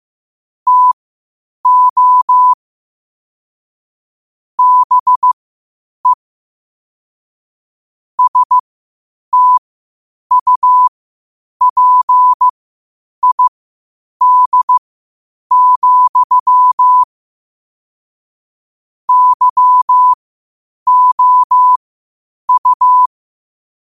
Morse code, heard as a keyed tone, about 15 wpm, spaced as for 8 wpm, 1000 Hz. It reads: TO BE STUPID, YOU
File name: qotdfiles / 8wpm